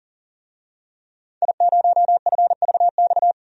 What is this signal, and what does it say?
Morse code, heard as a keyed tone, about 40 wpm, 700 Hz. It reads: I0FVX
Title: DXCC-100 Call Signs - 5 Characters - 40wpm